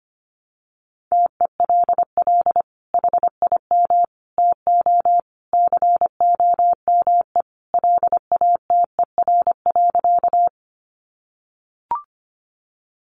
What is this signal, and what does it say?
Morse code, heard as a keyed tone, about 25 wpm, 700 Hz. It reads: TELL HIM TO COME LATER.
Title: Sentences from Top 700 Words 25wpm